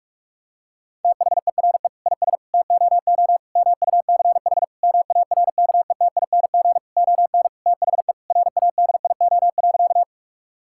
Morse code, heard as a keyed tone, about 46 wpm, 700 Hz. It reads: THERE IS TOO MUCH MARKETING ON THE RADIO.
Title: Sentences from Top 1000 Words - Review Part 5 - 45wpm